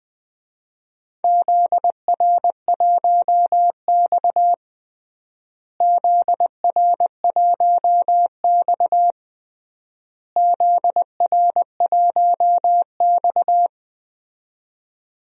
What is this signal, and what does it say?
Morse code, heard as a keyed tone, about 20 wpm, 700 Hz. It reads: ZR1X ZR1X ZR1X
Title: DXCC-100 Call Signs - 4 Characters - Repeated 3x - 20wpm